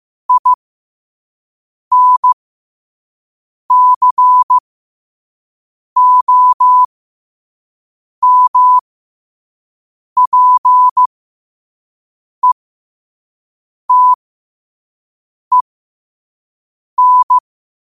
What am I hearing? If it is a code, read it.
Morse code, heard as a keyed tone, about 15 wpm, spaced as for 5 wpm, 1000 Hz. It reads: INCOMPETEN
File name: 5wpm